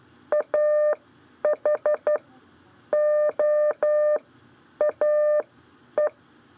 Morse code, heard as a keyed tone, about 12 wpm, spaced as for 8 wpm, 600 Hz. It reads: AHOAE
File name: 4 LEZIONE CW 1